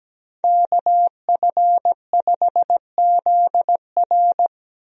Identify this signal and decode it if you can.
Morse code, heard as a keyed tone, about 17 wpm, 700 Hz. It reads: KF5ZR
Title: USA Call Signs - 5 Characters - Repeated 2x - 17wpm